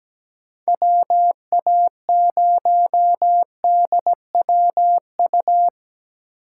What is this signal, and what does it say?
Morse code, heard as a keyed tone, about 17 wpm, 700 Hz. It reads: WA0DWU
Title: USA Call Signs - 6 Characters - 2x Character Spacing - 17wpm